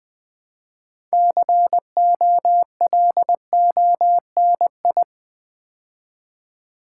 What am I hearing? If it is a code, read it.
Morse code, heard as a keyed tone, about 20 wpm, 700 Hz. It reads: COLONI
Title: Build A Word - 3 Letters At A Time - 8-Letter Words - 20wpm